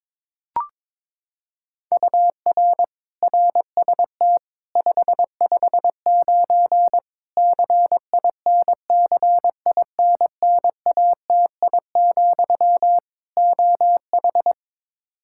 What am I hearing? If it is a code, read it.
Morse code, heard as a keyed tone, about 22 wpm, 700 Hz. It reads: UR RST 559 CINCINNATI, OH
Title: Practice QSO - Exchange 1 - Beginning - 22wpm